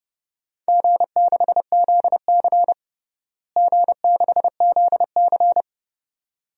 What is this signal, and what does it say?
Morse code, heard as a keyed tone, about 31 wpm, 700 Hz. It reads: G6ZC G6ZC
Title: DXCC-100 Call Signs - 4 Characters - Repeated 2x - 30wpm